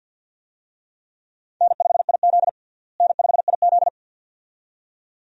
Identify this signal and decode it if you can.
Morse code, heard as a keyed tone, about 53 wpm, 700 Hz. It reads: N5IZ N5IZ
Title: USA Call Signs - 4 Characters - Repeated 2x - 50wpm